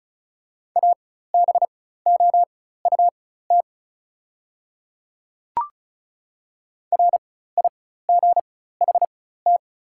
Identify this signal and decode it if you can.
Morse code, heard as a keyed tone, about 36 wpm, spaced as for 16 wpm, 690 Hz. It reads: ABOUT RIGHT
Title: 4x Character Spacing - Top 300 Words - 35wpm